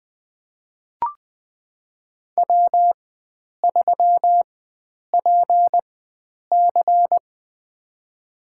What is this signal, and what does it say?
Morse code, heard as a keyed tone, about 20 wpm, spaced as for 9 wpm, 700 Hz. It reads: W3PC
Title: USA Call Signs - 4 Characters - 4x Character Spacing - 20wpm